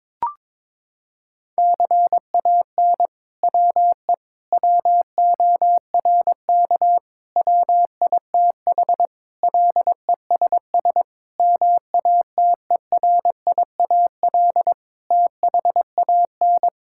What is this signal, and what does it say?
Morse code, heard as a keyed tone, about 22 wpm, 700 Hz. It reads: CAN WE WORK WITH LESS MATERIAL THAN
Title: Sentences from Top 1000 Words - Review Part 5 - 22wpm